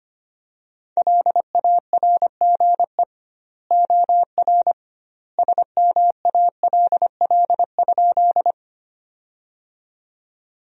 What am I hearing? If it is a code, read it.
Morse code, heard as a keyed tone, about 25 wpm, 700 Hz. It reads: LARGE OR SMALL?
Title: 2x Word-Spacing - Sentences from Top 200 Words - 25wpm